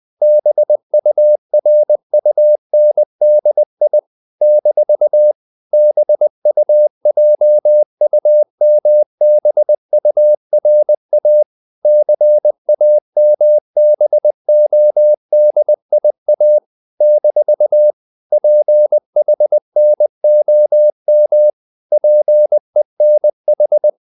Morse code, heard as a keyed tone, about 20 wpm, 600 Hz. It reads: BURUNDI - BUJUMBURA CAMBODIA - PHNOM PENH
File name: CW / Practice / NationalCapitals20